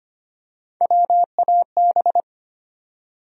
Morse code, heard as a keyed tone, about 25 wpm, 700 Hz. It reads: WAB
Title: USA Call Sign Postfixes - 3 Letters 25wpm